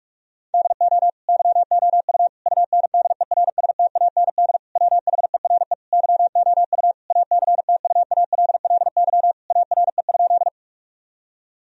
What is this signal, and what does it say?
Morse code, heard as a keyed tone, about 46 wpm, 700 Hz. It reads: DO YOU UNDERSTAND WHERE YOU ACTUALLY ARE?
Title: Sentences from Top 1000 Words - Review Part 4 - 45wpm